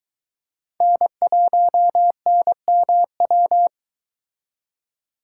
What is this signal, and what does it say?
Morse code, heard as a keyed tone, about 22 wpm, 700 Hz. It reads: N1NMW